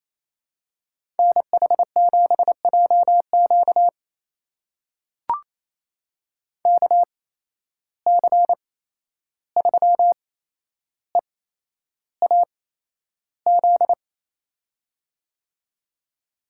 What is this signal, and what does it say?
Morse code, heard as a keyed tone, about 28 wpm, spaced as for 11 wpm, 700 Hz. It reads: NH7JQ KC3EAZ
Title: USA Call Signs - 8x Character Spacing - 28wpm